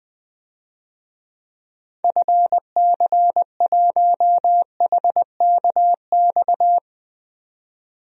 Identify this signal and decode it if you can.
Morse code, heard as a keyed tone, about 20 wpm, 700 Hz. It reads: FC1HKX
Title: DXCC-100 Call Signs - 6 Characters - 20wpm